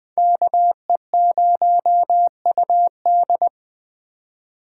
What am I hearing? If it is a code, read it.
Morse code, heard as a keyed tone, about 20 wpm, 700 Hz. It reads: KE0UD